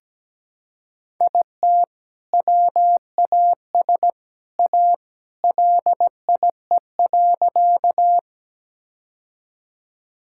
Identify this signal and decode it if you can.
Morse code, heard as a keyed tone, about 17 wpm, 700 Hz. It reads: IT WAS A LIE.
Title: Sentences from Top 1600 Words 17wpm